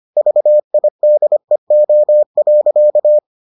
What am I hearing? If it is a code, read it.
Morse code, heard as a keyed tone, about 25 wpm, 600 Hz. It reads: VIDEO.